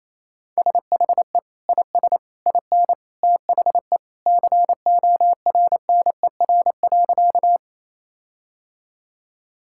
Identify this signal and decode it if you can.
Morse code, heard as a keyed tone, about 28 wpm, 700 Hz. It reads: SHE IS IN THE CORNER.
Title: Sentences from Top 1600 Words 28wpm